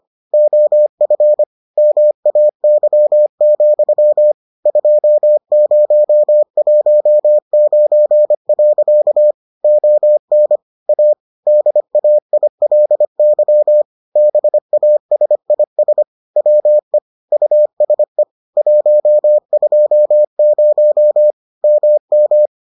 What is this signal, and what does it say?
Morse code, heard as a keyed tone, about 25 wpm, 600 Hz. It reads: OF MAY, 2019. ON A DAILY BASIS WE USE 120 MM